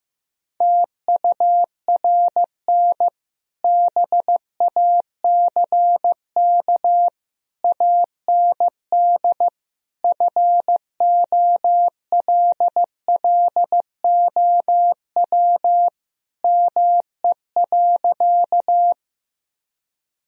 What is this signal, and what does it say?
Morse code, heard as a keyed tone, about 15 wpm, 700 Hz. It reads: TURN BACK AND FOLLOW ME.
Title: Sentences from Top 1000 Words - Review Part 1 - 15wpm